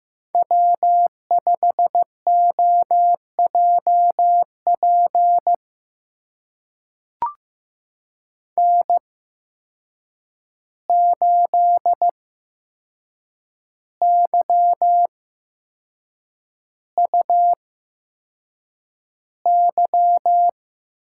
Morse code, heard as a keyed tone, about 15 wpm, spaced as for 6 wpm, 700 Hz. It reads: W5OJP N8YUY
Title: USA Call Signs - 8x Character Spacing - 15wpm